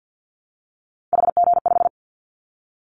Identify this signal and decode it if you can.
Morse code, heard as a keyed tone, about 50 wpm, 700 Hz. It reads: HG5
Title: Callsign Prefixes for Top 25 DXCC - 50wpm